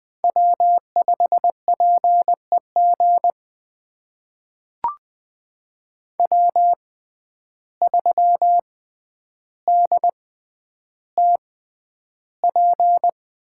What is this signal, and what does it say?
Morse code, heard as a keyed tone, about 20 wpm, spaced as for 10 wpm, 700 Hz. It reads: W5PEG W3DTP
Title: USA Call Signs - 6x Character Spacing - 20wpm